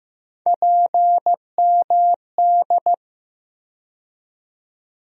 Morse code, heard as a keyed tone, about 15 wpm, 700 Hz. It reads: PMD
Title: USA Call Sign Postfixes - 3 Letters 15wpm